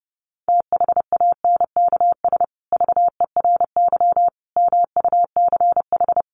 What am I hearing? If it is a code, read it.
Morse code, heard as a keyed tone, about 30 wpm, 700 Hz. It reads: THANKS VERY MUCH
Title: Sets of 5 Words Encore2 30wpm